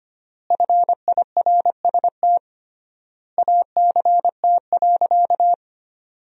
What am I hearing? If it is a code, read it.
Morse code, heard as a keyed tone, about 25 wpm, 700 Hz. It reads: FIRST ACT.
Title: 3x Word-Spacing - Sentences from Top 200 Words - 25wpm